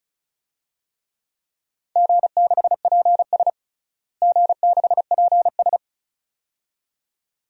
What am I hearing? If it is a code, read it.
Morse code, heard as a keyed tone, about 34 wpm, 690 Hz. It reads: G6PS G6PS
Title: DXCC-100 Call Signs - 4 Characters - Repeated 2x - 35wpm